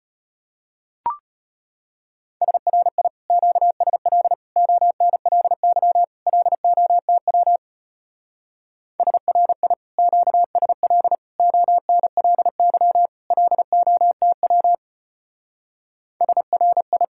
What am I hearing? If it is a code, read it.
Morse code, heard as a keyed tone, about 30 wpm, 700 Hz. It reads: SRI QSL ONLY LOTW SRI QSL ONLY LOTW SRI
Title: Speed-Racing Common QSO Phrases - 25wpm